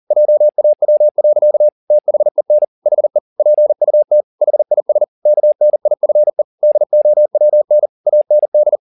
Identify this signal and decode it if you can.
Morse code, heard as a keyed tone, about 40 wpm, 600 Hz. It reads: JAW. THEN HE PUT HIS KNIFE DOWN AND
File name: the old man and the sea 2 40wpm